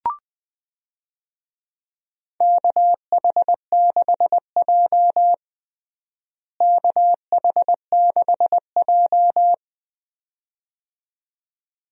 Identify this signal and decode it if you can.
Morse code, heard as a keyed tone, about 21 wpm, 700 Hz. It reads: KH6J KH6J
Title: DXCC-100 Call Signs - 4 Characters - Repeated 2x - 20wpm